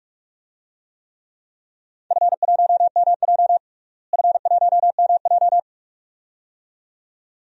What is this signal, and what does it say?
Morse code, heard as a keyed tone, about 46 wpm, 700 Hz. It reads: F1MJ F1MJ
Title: DXCC-100 Call Signs - 4 Characters - Repeated 2x - 45wpm